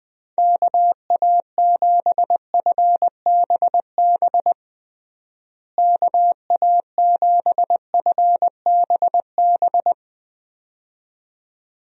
Morse code, heard as a keyed tone, about 20 wpm, 700 Hz. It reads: KA7FBB KA7FBB
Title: USA Call Signs - 6 Characters - Repeated 2x - 20wpm